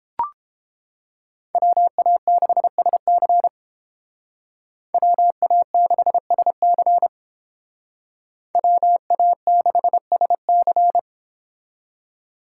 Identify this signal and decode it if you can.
Morse code, heard as a keyed tone, about 29 wpm, 700 Hz. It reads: WA6SC WA6SC WA6SC